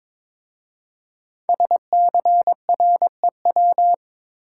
Morse code, heard as a keyed tone, about 22 wpm, 700 Hz. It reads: SCREW
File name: 5 Letter Words - First 3 Letters Given - 22wpm